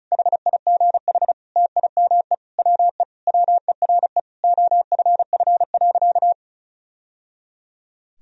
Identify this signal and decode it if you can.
Morse code, heard as a keyed tone, about 34 wpm, 700 Hz. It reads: HIGH TIME WE WERE OFF.
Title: Sentences from Top 1000 Words - Review Part 1 - 35wpm